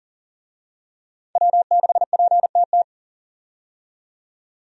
Morse code, heard as a keyed tone, about 40 wpm, 700 Hz. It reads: W6PTT
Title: USA Call Signs - 5 Characters - 40wpm